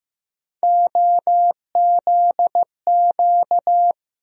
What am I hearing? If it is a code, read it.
Morse code, heard as a keyed tone, about 15 wpm, 700 Hz. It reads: OZQ
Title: USA Call Sign Postfixes - 3 Letters 15wpm